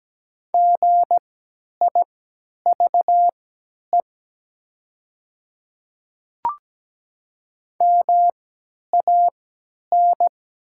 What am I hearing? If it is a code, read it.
Morse code, heard as a keyed tone, about 17 wpm, spaced as for 10 wpm, 700 Hz. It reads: GIVE MAN
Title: 3x Character Spacing - Top 200 Words - 17wpm